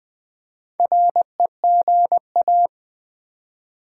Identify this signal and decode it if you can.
Morse code, heard as a keyed tone, about 20 wpm, 700 Hz. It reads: REGA
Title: Build A Word - 4 Letters At A Time - 10-Letter Words - 20wpm